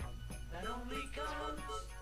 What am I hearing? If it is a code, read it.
There is no code.